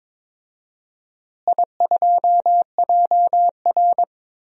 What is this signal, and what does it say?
Morse code, heard as a keyed tone, about 22 wpm, 700 Hz. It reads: I2JR
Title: DXCC-100 Call Signs - 4 Characters - Repeated 2x - 22wpm